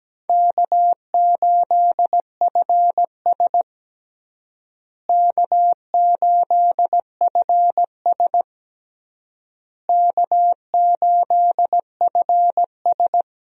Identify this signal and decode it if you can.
Morse code, heard as a keyed tone, about 17 wpm, 700 Hz. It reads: K8FS K8FS K8FS
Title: USA Call Signs - 4 Characters - Repeated 3x - 17wpm